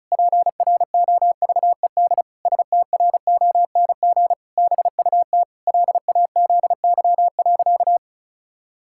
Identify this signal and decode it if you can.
Morse code, heard as a keyed tone, about 36 wpm, 700 Hz. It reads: PROVED STRONG BUT LAZY.
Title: Pangram Challenge v2 - 35wpm